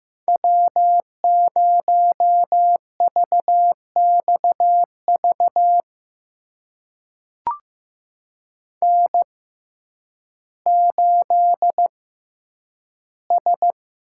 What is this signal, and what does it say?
Morse code, heard as a keyed tone, about 15 wpm, spaced as for 9 wpm, 700 Hz. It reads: W0VXV N8S
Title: USA Call Signs - 5 Characters - 6x Character Spacing - 15wpm